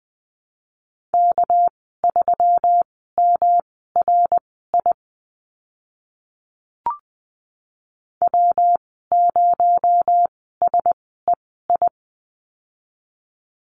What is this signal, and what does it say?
Morse code, heard as a keyed tone, about 20 wpm, spaced as for 14 wpm, 700 Hz. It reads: K3MRI W0SEI
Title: 219 Lesson 35g - Number 0 - Call Signs 20wpm